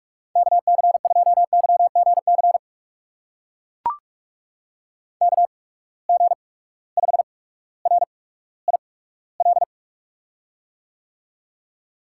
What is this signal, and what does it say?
Morse code, heard as a keyed tone, about 44 wpm, spaced as for 19 wpm, 700 Hz. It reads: KC2YGC KG5RIL
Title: USA Call Signs - 6 Characters - 8x Character Spacing - 45wpm